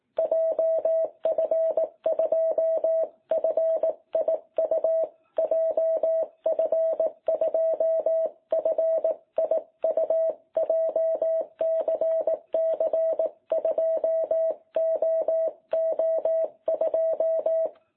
Morse code, heard as a keyed tone, about 18 wpm, 660 Hz. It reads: JF2FIU JF2FIU JCC2OO2
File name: jh8cbh / 2021cw